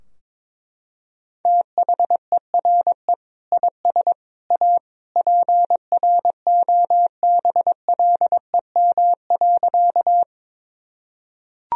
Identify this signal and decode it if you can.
Morse code, heard as a keyed tone, about 21 wpm, 700 Hz. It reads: THERE IS A PROBLEM.
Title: Sentences from Top 1000 Words - Review Part 2 - 22wpm